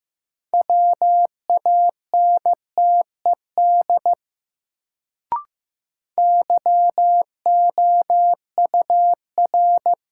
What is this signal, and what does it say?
Morse code, heard as a keyed tone, about 15 wpm, 700 Hz. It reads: WANTED YOUR
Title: Sets of 2 Words - New Word Drill - 15wpm